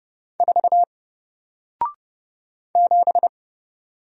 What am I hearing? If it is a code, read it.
Morse code, heard as a keyed tone, about 30 wpm, 700 Hz. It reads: 4 7
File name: Single-Letter-Number-30wpm